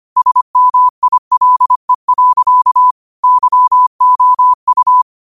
Morse code, heard as a keyed tone, about 25 wpm, 1000 Hz. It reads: SMILE. YOU
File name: qotdfiles / 25wpm